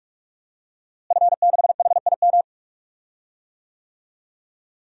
Morse code, heard as a keyed tone, about 41 wpm, 700 Hz. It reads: F6HIM
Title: DXCC-100 Call Signs - 5 Characters - Repeated 2x - 45wpm